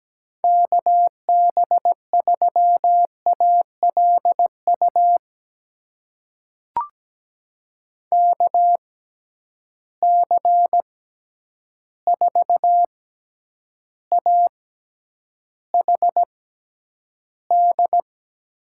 Morse code, heard as a keyed tone, about 17 wpm, spaced as for 9 wpm, 700 Hz. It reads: KB3ALU KC4AHD
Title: USA Call Signs - 6x Character Spacing - 17wpm